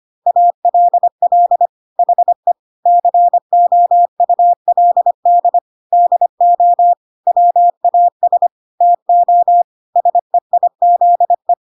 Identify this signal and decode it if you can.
Morse code, heard as a keyed tone, about 25 wpm, 700 Hz. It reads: ALL HE COULD DO WAS TO SEIZE
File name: Pinocchio / Pinocchio ch3 25 25 ik1hge